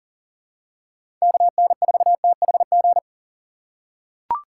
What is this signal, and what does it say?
Morse code, heard as a keyed tone, about 40 wpm, 700 Hz. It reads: KN4THG
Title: USA Call Signs - 6 Characters - 4x Character Spacing - 40wpm